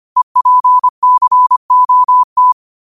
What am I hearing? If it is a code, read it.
Morse code, heard as a keyed tone, about 25 wpm, 1000 Hz. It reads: EPCOT
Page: Quotes for Fri, 15 Aug 2025 in Morse Code at 25 words per minute.